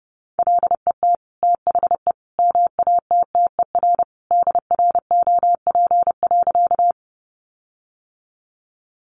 Morse code, heard as a keyed tone, about 30 wpm, 700 Hz. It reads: LET THE MATTER DROP.
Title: Sentences from Top 1400 Words 30wpm